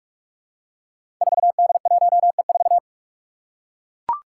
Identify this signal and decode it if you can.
Morse code, heard as a keyed tone, about 44 wpm, 700 Hz. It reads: 4D1E4
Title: Five Challenging Letter-Number Combinations 45wpm